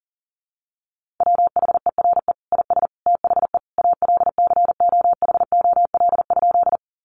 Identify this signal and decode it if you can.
Morse code, heard as a keyed tone, about 40 wpm, 700 Hz. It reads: WHERE IS THE ALCOHOL?
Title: Sentences from Top 1400 Words 40wpm